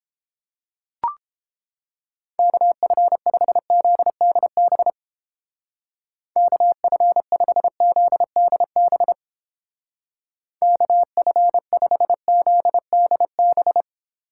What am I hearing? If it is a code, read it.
Morse code, heard as a keyed tone, about 29 wpm, 700 Hz. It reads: KF5ZDB KF5ZDB KF5ZDB